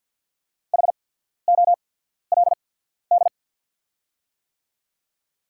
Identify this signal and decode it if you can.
Morse code, heard as a keyed tone, about 48 wpm, spaced as for 14 wpm, 690 Hz. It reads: HOLD
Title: 8x Character Spacing - Top 500 Words - 50wpm